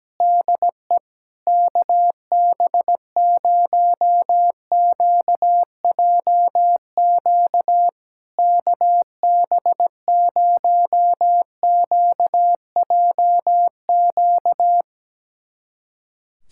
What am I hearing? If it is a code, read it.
Morse code, heard as a keyed tone, about 17 wpm, 700 Hz. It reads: DE KB0QJQ KB0QJQ
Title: SOTA QSO Practice 17wpm